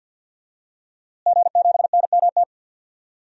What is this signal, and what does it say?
Morse code, heard as a keyed tone, about 52 wpm, 700 Hz. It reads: G7NMT
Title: DXCC-100 Call Signs - 5 Characters - Repeated 3x - 50wpm